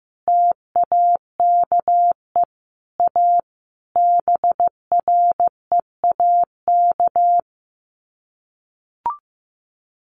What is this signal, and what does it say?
Morse code, heard as a keyed tone, about 15 wpm, 700 Hz. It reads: TAKE A BREAK
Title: Sets of 5 Words 15wpm